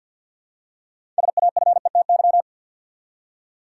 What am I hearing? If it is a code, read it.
Morse code, heard as a keyed tone, about 52 wpm, 700 Hz. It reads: SAFETY
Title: Six Letter Words - 50wpm